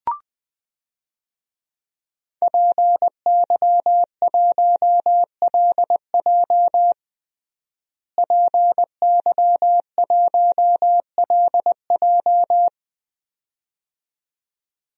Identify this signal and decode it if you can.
Morse code, heard as a keyed tone, about 20 wpm, 700 Hz. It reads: PY1LJ PY1LJ
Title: DXCC-100 Call Signs - 5 Characters - Repeated 2x - 20wpm